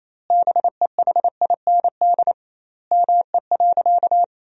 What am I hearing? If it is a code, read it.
Morse code, heard as a keyed tone, about 28 wpm, 700 Hz. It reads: BEHIND ME.
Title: 2x Word-Spacing - Sentences from Top 500 Words - 28wpm